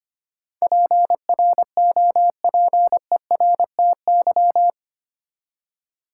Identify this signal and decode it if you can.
Morse code, heard as a keyed tone, about 25 wpm, 700 Hz. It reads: PROPERTY
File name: Eight Letter Words - 25wpm